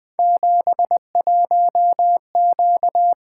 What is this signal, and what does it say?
Morse code, heard as a keyed tone, about 20 wpm, 700 Hz. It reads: 71Q
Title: Three Letter-Number Combinations - 20wpm